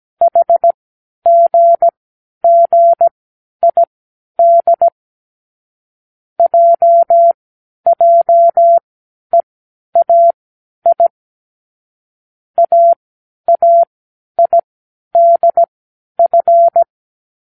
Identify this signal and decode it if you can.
Morse code, heard as a keyed tone, about 17 wpm, spaced as for 10 wpm, 680 Hz. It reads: HGGID JJEAI AAIDF
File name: k7qo / Lesson15(A-J)